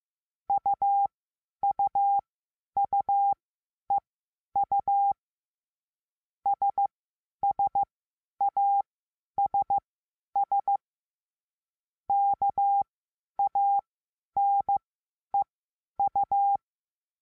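Morse code, heard as a keyed tone, about 15 wpm, spaced as for 10 wpm, 800 Hz. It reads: UUUEU SSASS KANEU
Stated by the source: Morse lesson adding the letter A, sending random five-letter practice groups.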